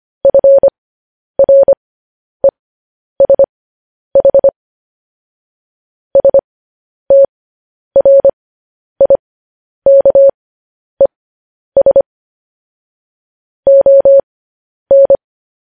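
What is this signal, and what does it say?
Morse code, heard as a keyed tone, about 25 wpm, spaced as for 10 wpm, 550 Hz. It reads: FRESH STRIKES ON
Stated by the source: Morse code news headlines transmission.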